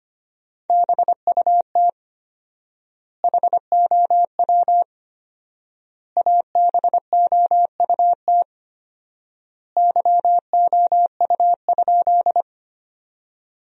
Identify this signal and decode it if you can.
Morse code, heard as a keyed tone, about 25 wpm, 700 Hz. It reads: BUT HOW ABOUT YOU?
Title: 4x Word-Spacing - Sentences from Top 100 Words - 25wpm